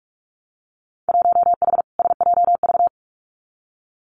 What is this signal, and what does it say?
Morse code, heard as a keyed tone, about 46 wpm, 700 Hz. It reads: 1H SJV
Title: Winter Field Day Exchanges 45wpm